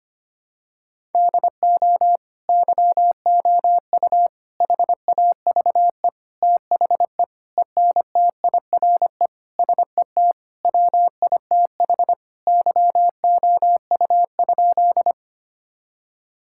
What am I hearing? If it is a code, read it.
Morse code, heard as a keyed tone, about 25 wpm, 700 Hz. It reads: DO YOU HAVE THE ENTIRE SET WITH YOU?
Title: Sentences from Top 1000 Words - Review Part 5 - 25wpm